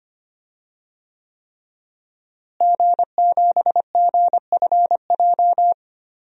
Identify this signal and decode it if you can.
Morse code, heard as a keyed tone, about 25 wpm, 700 Hz. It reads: G7GFJ
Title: DXCC-100 Call Signs - 5 Characters - Repeated 3x - 25wpm